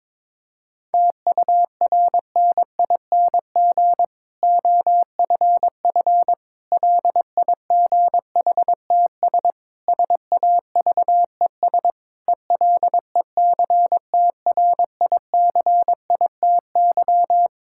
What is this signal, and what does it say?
Morse code, heard as a keyed tone, about 22 wpm, 700 Hz. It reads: TURNING OFF LIGHTS SAVES ELECTRICITY
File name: Difficult Sentences with Hints - 22wpm